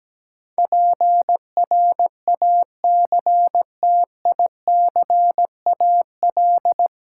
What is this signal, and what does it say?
Morse code, heard as a keyed tone, about 17 wpm, 700 Hz. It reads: PRACTICAL
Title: Sentences from Top 1400 Words - New Word Drill Alt - 17wpm